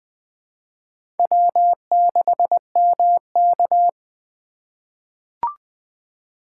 Morse code, heard as a keyed tone, about 21 wpm, 700 Hz. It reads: W6MK
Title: USA Call Signs - 4 Characters - 2x Character Spacing - 20wpm